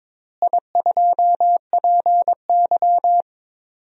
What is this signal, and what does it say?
Morse code, heard as a keyed tone, about 22 wpm, 700 Hz. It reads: I2PY
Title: DXCC-100 Call Signs - 4 Characters - Repeated 3x - 22wpm